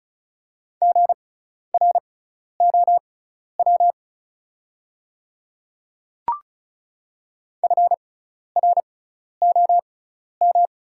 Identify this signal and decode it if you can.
Morse code, heard as a keyed tone, about 35 wpm, spaced as for 12 wpm, 690 Hz. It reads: GROW FROM
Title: 6x Character Spacing - Top 300 Words - 35wpm